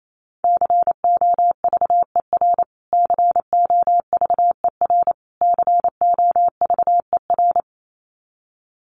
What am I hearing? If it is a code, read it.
Morse code, heard as a keyed tone, about 28 wpm, 700 Hz. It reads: COVER COVER COVER
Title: Top 300 Words Repeated 28wpm